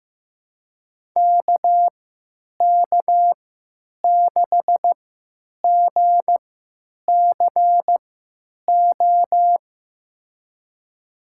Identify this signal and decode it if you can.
Morse code, heard as a keyed tone, about 15 wpm, spaced as for 9 wpm, 700 Hz. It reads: KK6GCO
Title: USA Call Signs - 6 Characters - 3x Character Spacing - 15wpm